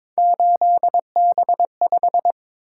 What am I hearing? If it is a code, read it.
Morse code, heard as a keyed tone, about 22 wpm, 700 Hz. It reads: 8B5
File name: practice / 22wpm / Three Letter-Number Combinations - 22wpm